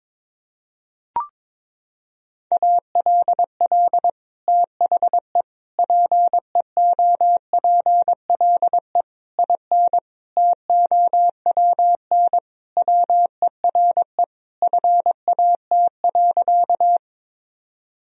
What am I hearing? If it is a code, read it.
Morse code, heard as a keyed tone, about 22 wpm, 700 Hz. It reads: ALL THE PEOPLE IN TOWN WERE FAT.
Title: Sentences from Top 1000 Words - Review Part 4 - 22wpm